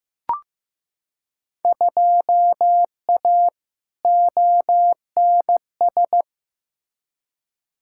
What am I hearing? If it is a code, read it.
Morse code, heard as a keyed tone, about 15 wpm, 700 Hz. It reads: 2A ONS
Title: Field Day Exchanges Encore - 15wpm